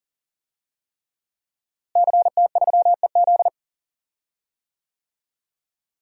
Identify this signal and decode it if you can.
Morse code, heard as a keyed tone, about 40 wpm, 700 Hz. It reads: CT3EZ
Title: DXCC-100 Call Signs - 5 Characters - Repeated 3x - 40wpm